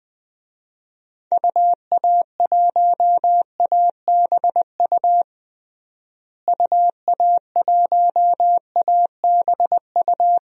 Morse code, heard as a keyed tone, about 20 wpm, 700 Hz. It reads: UA1ABU UA1ABU
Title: DXCC-100 Call Signs - 6 Characters - Repeated 2x - 20wpm